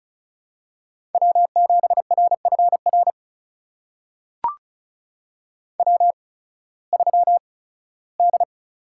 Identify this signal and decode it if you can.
Morse code, heard as a keyed tone, about 36 wpm, spaced as for 19 wpm, 700 Hz. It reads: W7RFR W3D
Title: USA Call Signs - 8x Character Spacing - 35wpm